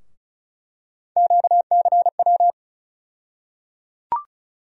Morse code, heard as a keyed tone, about 34 wpm, 700 Hz. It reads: QCW
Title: USA Call Sign Postfixes - 3 Letters 35wpm